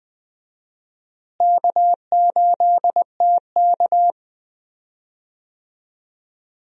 Morse code, heard as a keyed tone, about 21 wpm, 700 Hz. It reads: K8TK
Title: USA Call Signs - 4 Characters - Repeated 3x - 20wpm